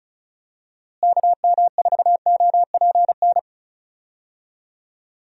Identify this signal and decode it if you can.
Morse code, heard as a keyed tone, about 34 wpm, 700 Hz. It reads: KM4OPN
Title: USA Call Signs - 6 Characters - Repeated 2x - 35wpm